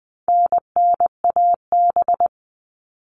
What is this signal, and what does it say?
Morse code, heard as a keyed tone, about 20 wpm, 700 Hz. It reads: NNAB
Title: All Cut Numbers - 4-Digit 20wpm